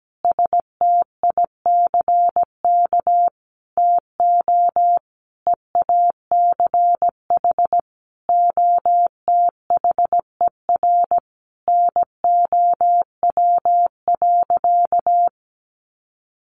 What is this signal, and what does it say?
Morse code, heard as a keyed tone, about 17 wpm, 700 Hz. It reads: STICK TO EACH OTHER NOW.
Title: Sentences from Top 1200 Words 17wpm